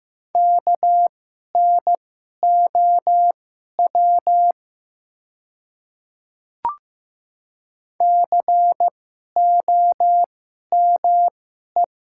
Morse code, heard as a keyed tone, about 15 wpm, spaced as for 11 wpm, 700 Hz. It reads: KNOW COME
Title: 2x Character Spacing - Top 100 Words - 15wpm